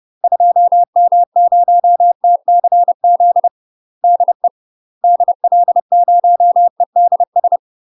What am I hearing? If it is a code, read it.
Morse code, heard as a keyed tone, about 30 wpm, 700 Hz. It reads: 2M0TCZ DE DL0EDS
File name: UK MASTER QSOs TRAINING/30WPM